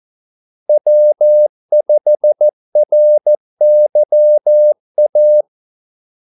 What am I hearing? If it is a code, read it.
Morse code, heard as a keyed tone, about 14 wpm, 600 Hz. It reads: W5RYA
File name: Callsigns-14wpm